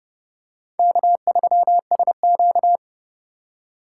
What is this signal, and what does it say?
Morse code, heard as a keyed tone, about 31 wpm, 700 Hz. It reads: K3SQ